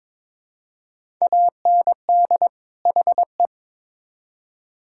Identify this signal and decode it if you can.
Morse code, heard as a keyed tone, about 22 wpm, 700 Hz. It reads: AND HE
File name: Build A Phrase - 2-Word Phrases - 22wpm